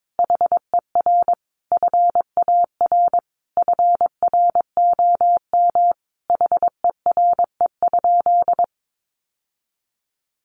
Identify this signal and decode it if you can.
Morse code, heard as a keyed tone, about 22 wpm, 700 Hz. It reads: HER FAR FROM HERE?
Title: Sentences from Top 1100 Words 22wpm